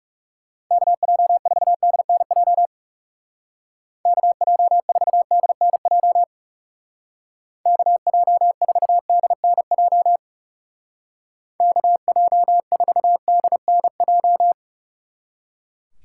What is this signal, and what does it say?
Morse code, heard as a keyed tone, about 35 wpm, 700 Hz. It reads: KJ4DNJ KJ4DNJ KJ4DNJ KJ4DNJ
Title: Speed Racing - USA 2x3 Call Signs 30wpm